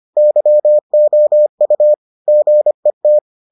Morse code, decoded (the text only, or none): YOU GET